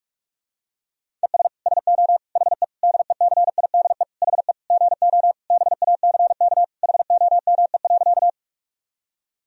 Morse code, heard as a keyed tone, about 46 wpm, 690 Hz. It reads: ES SO HE DECIDE HE GO BACK HOME.
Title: Sentences from Top 500 Words and Abbreviations 45wpm